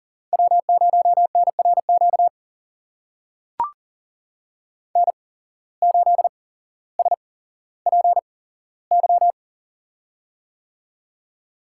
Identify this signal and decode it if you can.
Morse code, heard as a keyed tone, about 38 wpm, spaced as for 17 wpm, 700 Hz. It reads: W0NRQ N8SPY
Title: USA Call Signs - 5 Characters - 8x Character Spacing - 40wpm